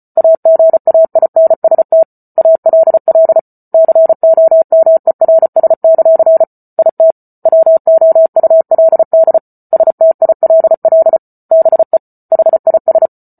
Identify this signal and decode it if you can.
Morse code, heard as a keyed tone, about 34 wpm, 650 Hz. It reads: AGAINST ALL COMERS; IT WOULD STILL BE HIS
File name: Book Two / 08-Chapter 8 - 34 WPM